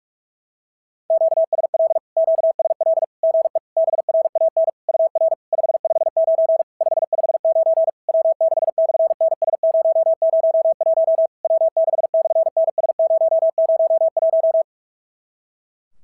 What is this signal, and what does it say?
Morse code, heard as a keyed tone, about 48 wpm, 650 Hz. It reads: QSL QSL GE BRAN UR 559 559 W6/NS001 W6/NS001
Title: SOTA QSO Practice 45wpm